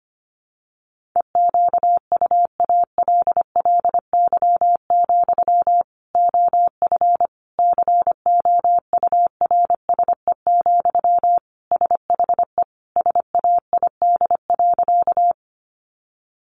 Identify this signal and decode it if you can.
Morse code, heard as a keyed tone, about 25 wpm, 700 Hz. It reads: EQUALLY, OF COURSE, SHE SAID.
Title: Sentences from Top 1400 Words 25wpm